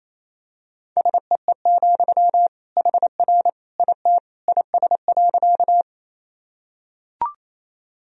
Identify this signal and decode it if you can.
Morse code, heard as a keyed tone, about 28 wpm, 700 Hz. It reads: SEE, HR IT IS.
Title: Sentences from Top 200 Words and Abbreviations 28wpm